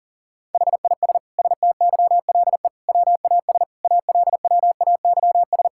Morse code, heard as a keyed tone, about 40 wpm, 700 Hz. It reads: HIS STYLE WAS ALWAYS
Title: Sentences from Top 1000 Words - Review Part 5 - 40wpm